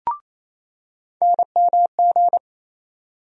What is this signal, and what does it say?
Morse code, heard as a keyed tone, about 28 wpm, 700 Hz. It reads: NMG
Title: USA Call Sign Postfixes - 3 Letters 28wpm